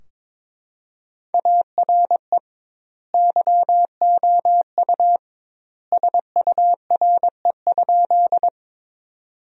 Morse code, heard as a keyed tone, about 22 wpm, 700 Hz. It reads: ARE YOU SURE?
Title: 2x Word-Spacing - Sentences from Top 300 Words - 22wpm